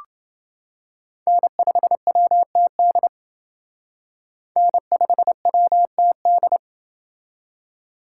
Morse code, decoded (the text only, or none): N5WTD N5WTD